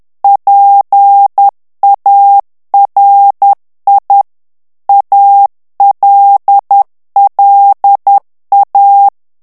Morse code, heard as a keyed tone, about 10 wpm, 800 Hz. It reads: PARI ALLA
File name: ITALIEN texte clair 2 vitesse 10 mots minute